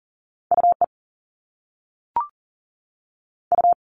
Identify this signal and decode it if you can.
Morse code, heard as a keyed tone, about 38 wpm, 700 Hz. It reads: UE U
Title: Top Two Letter Combinations in Words 40wpm